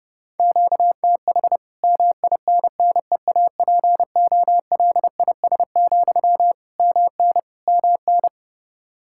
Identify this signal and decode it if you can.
Morse code, heard as a keyed tone, about 32 wpm, 700 Hz. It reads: QTH MINNEAPOLIS, MN MN
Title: Practice QSO - Exchange 1 - Beginning - 30wpm